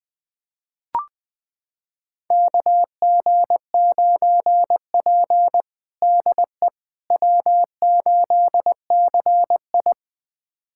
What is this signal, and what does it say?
Morse code, heard as a keyed tone, about 20 wpm, 700 Hz. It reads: KG9P DE W8CI
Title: USA Callsign DE Callsign - 4 Characters - 20wpm